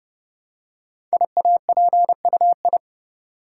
Morse code, heard as a keyed tone, about 30 wpm, 700 Hz. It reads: IAPUI
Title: Code Groups - 5 Letters - 30wpm